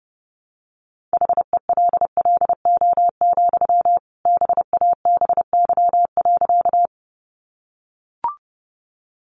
Morse code, heard as a keyed tone, about 30 wpm, 700 Hz. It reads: HELLO, BABY.
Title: Sentences from Top 1300 Words 30wpm